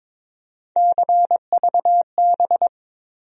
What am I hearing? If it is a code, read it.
Morse code, heard as a keyed tone, about 22 wpm, 700 Hz. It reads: CVB
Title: USA Call Sign Postfixes - 3 Letters 22wpm